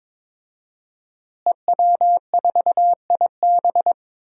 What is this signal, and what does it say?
Morse code, heard as a keyed tone, about 21 wpm, 700 Hz. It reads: EW4IB
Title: DXCC-100 Call Signs - 5 Characters - Repeated 3x - 22wpm